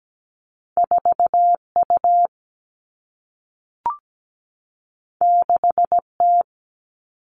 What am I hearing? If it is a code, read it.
Morse code, heard as a keyed tone, about 17 wpm, 700 Hz. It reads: 4U 6T
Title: All Cut Numbers - 2-Digit 17wpm